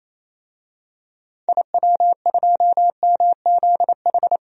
Morse code, heard as a keyed tone, about 27 wpm, 700 Hz. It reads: IW2MZH
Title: DXCC-100 Call Signs - 6 Characters - Repeated 3x - 28wpm